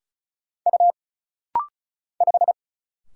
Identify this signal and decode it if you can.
Morse code, heard as a keyed tone, about 36 wpm, 700 Hz. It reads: U 5
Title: Single Letter-Number Alt - Rapid-Fire 35wpm